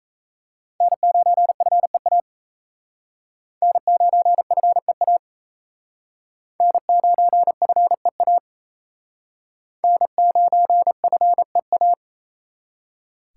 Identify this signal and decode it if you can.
Morse code, heard as a keyed tone, about 36 wpm, 700 Hz. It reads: N9FEA N9FEA N9FEA N9FEA